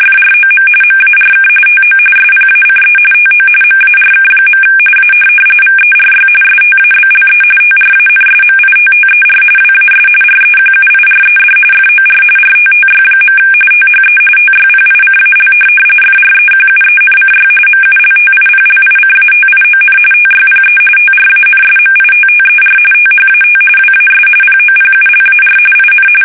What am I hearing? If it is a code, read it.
RTTY, about 45.45 baud. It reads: FROM MARK TO SPACE TO MARK. A VERY STABLE TRANSMITTER WAS REQUIRED FOR RTTY. THE TYPICAL FREQUENCY MULTIPLICATION TYPE TRANSMITTER THAT WAS POPULAR IN
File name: fsk850 / teletype-RTTY